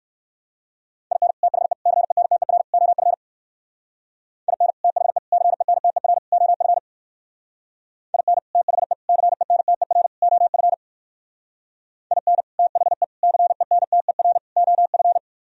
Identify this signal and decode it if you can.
Morse code, heard as a keyed tone, about 59 wpm, 700 Hz. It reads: IN THE CENTER OF IN THE CENTER OF IN THE CENTER OF IN THE CENTER OF